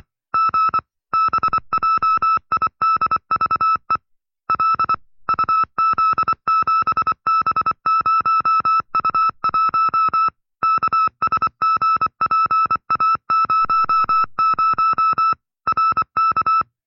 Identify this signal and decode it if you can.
Morse code, heard as a keyed tone, about 24 wpm, 1310 Hz. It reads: G BJIDVE L UZ7B0U1 KSGPA00 RK
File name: FreeFiles / CW-practice / 20WPM-side2